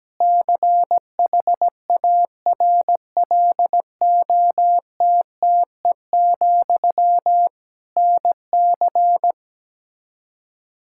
Morse code, heard as a keyed tone, about 17 wpm, 700 Hz. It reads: CHARLOTTE, NC
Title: Practice QSO - Exchange 1 - Beginning - 17wpm